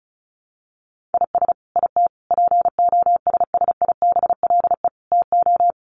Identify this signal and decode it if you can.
Morse code, heard as a keyed tone, about 34 wpm, 700 Hz. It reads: IS IT POSSIBLE TO
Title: Sentences from Top 1200 Words 35wpm